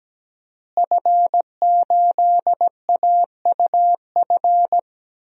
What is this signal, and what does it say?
Morse code, heard as a keyed tone, about 17 wpm, 700 Hz. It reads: F8AUF
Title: DXCC-100 Call Signs - 5 Characters - Repeated 2x - 17wpm